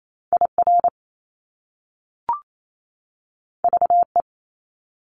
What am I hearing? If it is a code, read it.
Morse code, heard as a keyed tone, about 28 wpm, 700 Hz. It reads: IR VE
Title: Top Two Letter Combinations in Words 28wpm